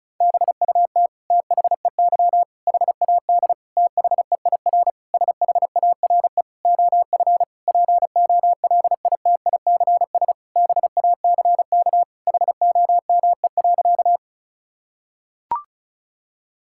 Morse code, heard as a keyed tone, about 36 wpm, 700 Hz. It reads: BUT THEY HAD THEIR SHARE OF POLITICS BACK HOME.
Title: Sentences from Top 1000 Words - Review Part 6 - 35wpm